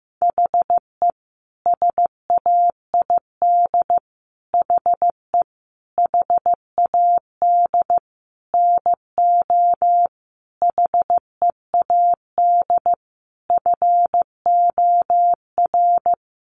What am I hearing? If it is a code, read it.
Morse code, heard as a keyed tone, about 15 wpm, 700 Hz. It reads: HE SAID HE HAD NO HEAD FOR
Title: Sentences from Top 1500 Words 15wpm